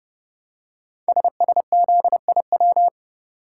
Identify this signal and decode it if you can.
Morse code, heard as a keyed tone, about 30 wpm, 700 Hz. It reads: SSZIW
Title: Code Groups - 5 Letters - 30wpm